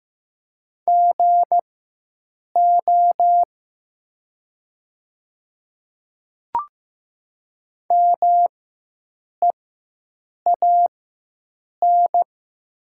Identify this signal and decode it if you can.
Morse code, heard as a keyed tone, about 15 wpm, spaced as for 7 wpm, 700 Hz. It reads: GO MEAN
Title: 4x Character Spacing - Top 300 Words - 15wpm